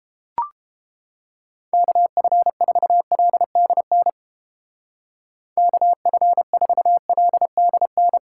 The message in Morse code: KF4LDN KF4LDN

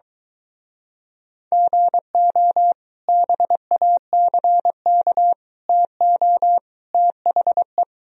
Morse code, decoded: GO BACK TO THE

